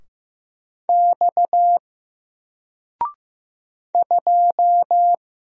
Morse code, heard as a keyed tone, about 15 wpm, 700 Hz. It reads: X 2
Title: Single Letter-Number 15wpm